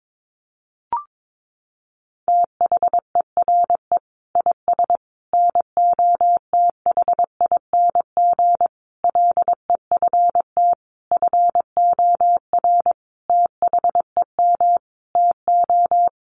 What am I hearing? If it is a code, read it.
Morse code, heard as a keyed tone, about 22 wpm, 700 Hz. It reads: THERE IS NOTHING LEFT FOR THEM TO
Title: Sentences from Top 1100 Words 22wpm